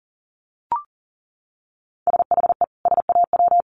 Morse code, heard as a keyed tone, about 37 wpm, 700 Hz. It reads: SHE SAW
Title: Sentences from Top 300 Words 40wpm